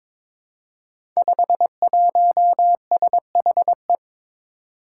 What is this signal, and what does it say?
Morse code, heard as a keyed tone, about 22 wpm, 700 Hz. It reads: 51SHE
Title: Five Challenging Letter-Number Combinations 22wpm